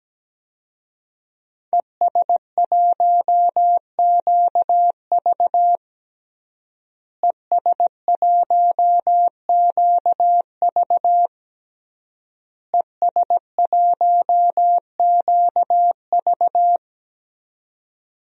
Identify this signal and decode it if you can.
Morse code, heard as a keyed tone, about 17 wpm, 700 Hz. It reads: ES1QV ES1QV ES1QV
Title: DXCC-100 Call Signs - 5 Characters - Repeated 3x - 17wpm